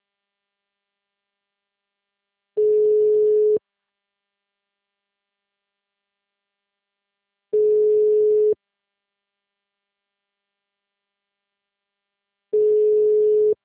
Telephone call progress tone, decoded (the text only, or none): ringback tone